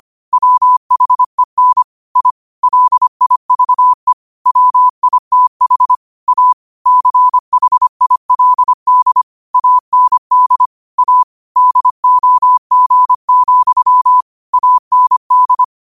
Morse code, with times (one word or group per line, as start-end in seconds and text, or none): 0.3-1.8 WHEN
2.2-2.3 I
2.6-4.1 LIVE
4.5-6.0 WITH
6.3-6.5 A
6.9-9.2 CHILD
9.5-10.7 AND
11.0-11.2 A
11.6-14.2 DOG,
14.5-15.6 AND